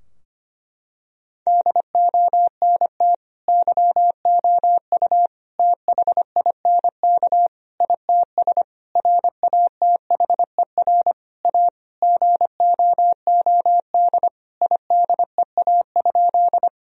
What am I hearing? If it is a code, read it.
Morse code, heard as a keyed tone, about 25 wpm, 700 Hz. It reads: DONT YOU THINK ITS RATHER A GOOD IDEA?
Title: Sentences from Top 1000 Words - Review Part 4 - 25wpm